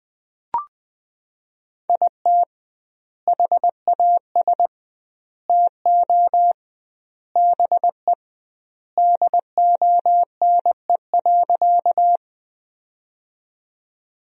Morse code, heard as a keyed tone, about 21 wpm, 700 Hz. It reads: IT HAS TO BE DONE.
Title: 2x Word-Spacing - Sentences from Top 500 Words - 20wpm